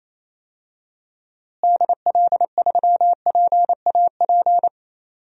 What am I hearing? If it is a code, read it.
Morse code, heard as a keyed tone, about 27 wpm, 700 Hz. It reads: DL3PAP
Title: DXCC-100 Call Signs - 6 Characters - Repeated 2x - 28wpm